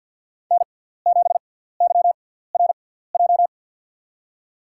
Morse code, heard as a keyed tone, about 47 wpm, spaced as for 17 wpm, 700 Hz. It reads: N7YRJ
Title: USA Call Signs - 5 Characters - 6x Character Spacing - 50wpm